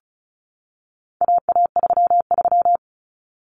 Morse code, heard as a keyed tone, about 34 wpm, 700 Hz. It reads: AA33
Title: Common Cut Numbers - 4-Digit 35wpm